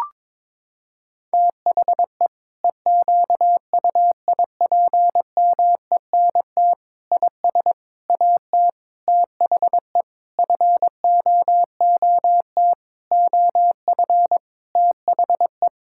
Morse code, heard as a keyed tone, about 22 wpm, 700 Hz. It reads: THE EQUIPMENT IS AT THE FOOT OF THE
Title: Sentences from Top 1000 Words - Review Part 5 - 22wpm